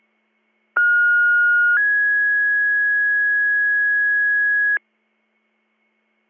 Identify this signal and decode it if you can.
Two-tone page, heard as a one-second tone, then a three-second tone, about 1430 then 1650 Hz. It is heard from 0.8 to 4.8 s.